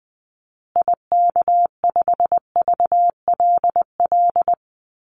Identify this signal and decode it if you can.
Morse code, heard as a keyed tone, about 20 wpm, 700 Hz. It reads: IK5VLL
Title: Call Signs Repeated 2x - 20wpm